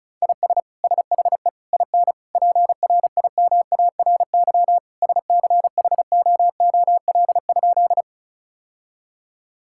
Morse code, heard as a keyed tone, about 36 wpm, 700 Hz. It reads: IS SHE IN PRIMARY SCHOOL?
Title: Sentences from Top 1000 Words - Review Part 5 - 35wpm